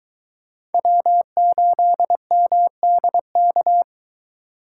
Morse code, heard as a keyed tone, about 22 wpm, 700 Hz. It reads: W8MDK